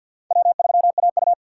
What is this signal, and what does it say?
Morse code, heard as a keyed tone, about 49 wpm, 700 Hz. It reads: W3AU
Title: USA Call Signs - 2x Character Spacing - 50wpm